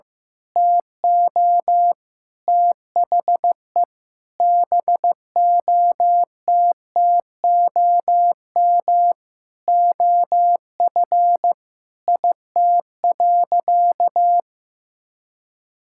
Morse code, heard as a keyed tone, about 14 wpm, 700 Hz. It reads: TO THE BOTTOM OF IT.
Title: Sentences from Top 1000 Words - Review Part 6 - 15wpm